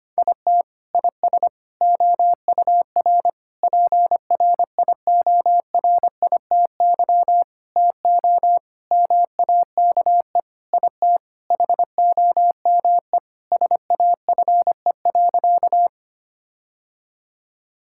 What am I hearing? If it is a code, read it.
Morse code, heard as a keyed tone, about 25 wpm, 700 Hz. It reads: IT IS OUR PRIORITY TO MAKE IT HOME SAFE.
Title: Sentences from Top 1600 Words 25wpm